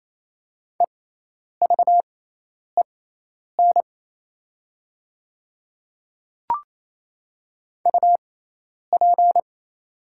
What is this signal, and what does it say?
Morse code, heard as a keyed tone, about 27 wpm, spaced as for 10 wpm, 700 Hz. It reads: EVEN UP
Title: 6x Character Spacing - Top 300 Words - 28wpm